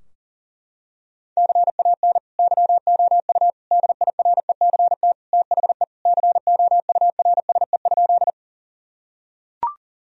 Morse code, heard as a keyed tone, about 40 wpm, 700 Hz. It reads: CAN YOU DIRECT THE COURSE?
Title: Sentences from Top 1000 Words - Review Part 2 - 40wpm